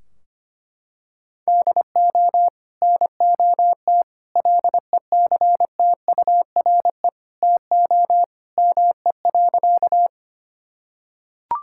Morse code, heard as a keyed tone, about 25 wpm, 700 Hz. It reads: DO NOT LECTURE TO ME.
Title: Sentences from Top 1600 Words 25wpm